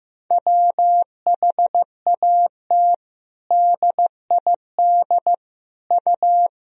Morse code, heard as a keyed tone, about 15 wpm, 700 Hz. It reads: WHAT DID U
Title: Sentences from Top 300 Words and Abbreviations 15wpm